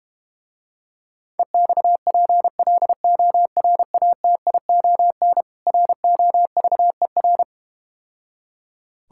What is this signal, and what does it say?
Morse code, heard as a keyed tone, about 32 wpm, 700 Hz. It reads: EXPLORATION ROVER